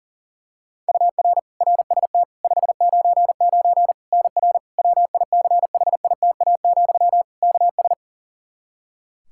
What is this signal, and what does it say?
Morse code, heard as a keyed tone, about 40 wpm, 700 Hz. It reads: UR RST 599 NR WICHITA, KS
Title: K1USN SST QSO Practice 40wpm